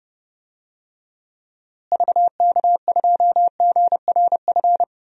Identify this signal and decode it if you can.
Morse code, heard as a keyed tone, about 30 wpm, 700 Hz. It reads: VK2GRF
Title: DXCC-100 Call Signs - 6 Characters - Repeated 3x - 30wpm